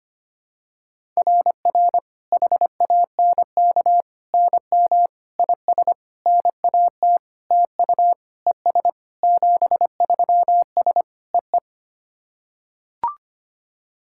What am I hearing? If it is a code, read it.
Morse code, heard as a keyed tone, about 25 wpm, 700 Hz. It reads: RR HANK NM IS NAT TU ES 73S EE
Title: SOTA QSO Practice 25wpm